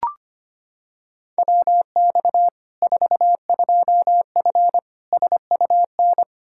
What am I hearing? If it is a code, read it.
Morse code, heard as a keyed tone, about 25 wpm, 700 Hz. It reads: WX 42F SUN
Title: K1USN SST QSO Practice 25wpm